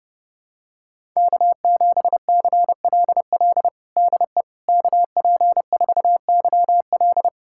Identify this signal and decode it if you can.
Morse code, heard as a keyed tone, about 31 wpm, 700 Hz. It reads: K7CLL DE KP4YL
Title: USA Callsign DE Callsign - 5 Characters - 30wpm